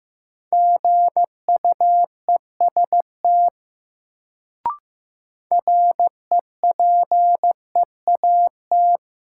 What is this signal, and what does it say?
Morse code, heard as a keyed tone, about 15 wpm, 700 Hz. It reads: GUEST REPEAT
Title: Sentences from Top 1600 Words - New Word Drill - 15wpm